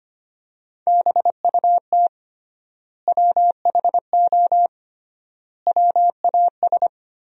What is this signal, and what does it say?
Morse code, heard as a keyed tone, about 25 wpm, 700 Hz. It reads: BUT WHO WAS
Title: 3x Word-Spacing - Sentences from Top 100 Words - 25wpm